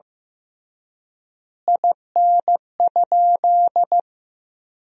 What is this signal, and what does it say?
Morse code, heard as a keyed tone, about 15 wpm, 700 Hz. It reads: IN?
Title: 3x Word-Spacing - Sentences from Top 300 Words - 15wpm